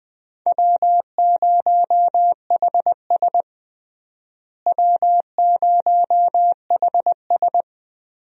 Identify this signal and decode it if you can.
Morse code, heard as a keyed tone, about 20 wpm, 700 Hz. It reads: W0HS W0HS
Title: USA Call Signs - 4 Characters - Repeated 2x - 20wpm